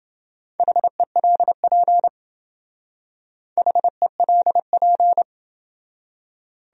Morse code, heard as a keyed tone, about 28 wpm, 700 Hz. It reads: HELP HELP